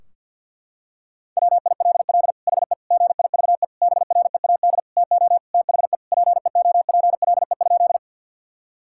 Morse code, heard as a keyed tone, about 50 wpm, 700 Hz. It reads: WILL HE GIVE BREAD TO THE PEOPLE?
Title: Sentences from Top 1600 Words 50wpm